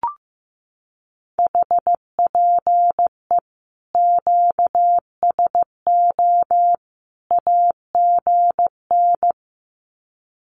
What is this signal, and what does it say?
Morse code, heard as a keyed tone, about 15 wpm, 700 Hz. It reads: HPE QSO AGN
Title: Common QSO Phrases 15wpm